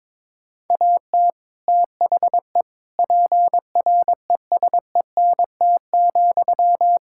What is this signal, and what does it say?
Morse code, heard as a keyed tone, about 22 wpm, 700 Hz. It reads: AT THE PRESENT,
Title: Sentences from Top 1000 Words - Review Part 5 - 22wpm